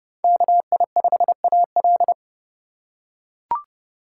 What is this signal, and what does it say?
Morse code, heard as a keyed tone, about 30 wpm, 700 Hz. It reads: KI5AL
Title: K1USN SST QSO Practice - 30wpm